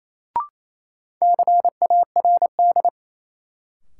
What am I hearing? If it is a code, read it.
Morse code, heard as a keyed tone, about 28 wpm, 700 Hz. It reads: CARD
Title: Sentences from Top 600 Words - New Word Drill - 28wpm